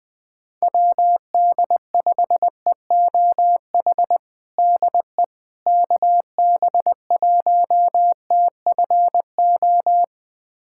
Morse code, heard as a keyed tone, about 20 wpm, 700 Hz. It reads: WD5EOH DE KB1TFO
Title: USA Callsign DE Callsign - 6 Characters - 20wpm